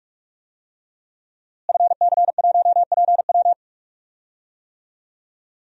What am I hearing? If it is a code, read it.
Morse code, heard as a keyed tone, about 46 wpm, 700 Hz. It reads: FC1PW
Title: DXCC-100 Call Signs - 5 Characters - Repeated 3x - 45wpm